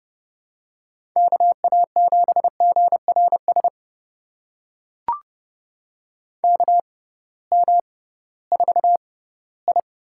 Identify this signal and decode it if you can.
Morse code, heard as a keyed tone, about 30 wpm, spaced as for 17 wpm, 700 Hz. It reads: KA7GRS KM4I
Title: USA Call Signs - 6 Characters - 6x Character Spacing - 30wpm